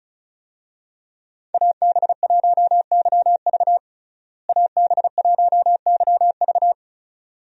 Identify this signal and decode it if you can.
Morse code, heard as a keyed tone, about 34 wpm, 700 Hz. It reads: AB1YV AB1YV
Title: USA Call Signs - 5 Characters - Repeated 2x - 35wpm